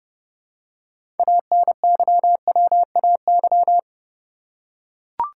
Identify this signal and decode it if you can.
Morse code, heard as a keyed tone, about 30 wpm, 700 Hz. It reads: ANYWAY
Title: Six Letter Words - 30wpm